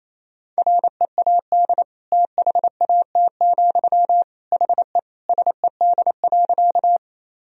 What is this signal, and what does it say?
Morse code, heard as a keyed tone, about 28 wpm, 700 Hz. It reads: READ THAT, HE SED.
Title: Sentences from Top 200 Words and Abbreviations 28wpm